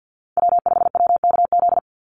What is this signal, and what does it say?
Morse code, heard as a keyed tone, about 50 wpm, 700 Hz. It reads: W5WKZ
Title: USA Call Signs 50wpm